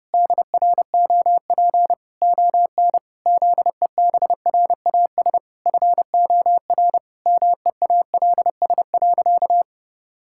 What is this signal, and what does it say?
Morse code, heard as a keyed tone, about 30 wpm, 700 Hz. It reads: DROP ON ZEBRAS FOR MEALS.
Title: Pangram Challenge v2 - 30wpm